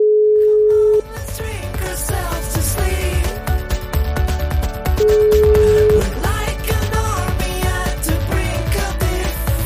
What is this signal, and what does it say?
Ringback tone (Europe, 425 Hz).